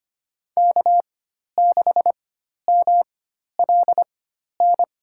Morse code, heard as a keyed tone, about 25 wpm, spaced as for 12 wpm, 700 Hz. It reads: K6MLN
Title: USA Call Signs - 5 Characters - 4x Character Spacing - 25wpm